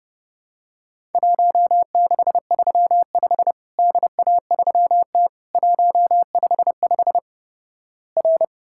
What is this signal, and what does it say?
Morse code, heard as a keyed tone, about 32 wpm, 700 Hz. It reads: 1635 DA3T 155 R
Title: Worked All Europe DX Contest - QTC Exchange - 30wpm